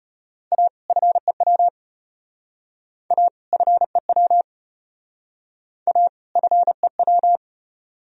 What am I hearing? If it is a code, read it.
Morse code, heard as a keyed tone, about 31 wpm, 700 Hz. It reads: A FEW A FEW A FEW
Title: Speed-Racing - Sets of 2 Words 25wpm